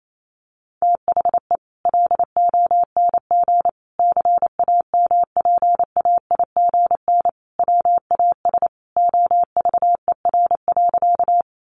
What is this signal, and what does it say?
Morse code, heard as a keyed tone, about 28 wpm, 700 Hz. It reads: THE LONG CAMPAIGN WAS OVER.
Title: Sentences from Top 1200 Words 28wpm